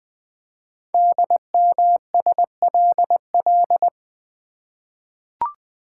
Morse code, heard as a keyed tone, about 20 wpm, 700 Hz. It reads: DMSLL
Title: Code Groups - 5 Letter:Numbers - 20wpm